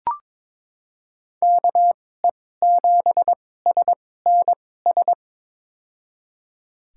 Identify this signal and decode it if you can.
Morse code, heard as a keyed tone, about 23 wpm, spaced as for 16 wpm, 700 Hz. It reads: KE7SNS
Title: USA Call Signs - 6 Characters - 2x Character Spacing - 22wpm